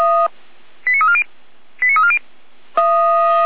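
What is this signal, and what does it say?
Five-tone selective call (ZVEI), heard as quick runs of five tones, tones of about 70 ms. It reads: 89340 89340
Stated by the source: ZVEI five-tone selective call 89265 received by SDR.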